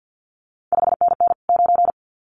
Morse code, heard as a keyed tone, about 50 wpm, 700 Hz. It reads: 5NN 9
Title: CQ WW DX Contest Exchange - 50wpm